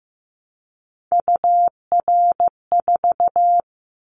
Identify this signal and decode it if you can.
Morse code, heard as a keyed tone, about 15 wpm, 700 Hz. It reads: UR4
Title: Callsign Prefixes for Top 25 DXCC - 15wpm